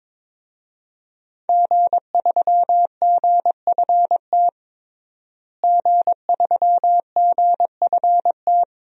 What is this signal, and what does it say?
Morse code, heard as a keyed tone, about 21 wpm, 700 Hz. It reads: G3GFT G3GFT
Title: DXCC-100 Call Signs - 5 Characters - Repeated 2x - 22wpm